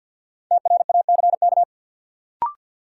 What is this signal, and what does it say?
Morse code, heard as a keyed tone, about 47 wpm, 700 Hz. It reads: TRACK
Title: Sentences from Top 1100 Words - New Word Drill - 50wpm